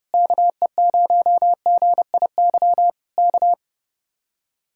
Morse code, heard as a keyed tone, about 28 wpm, 700 Hz. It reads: KE0GIY K
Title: Speed-Racing Common QSO Phrases - 30wpm